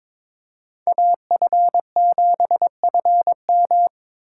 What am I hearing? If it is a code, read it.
Morse code, heard as a keyed tone, about 22 wpm, 700 Hz. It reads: AF7FM
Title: USA Call Signs - 4x Character Spacing - 22wpm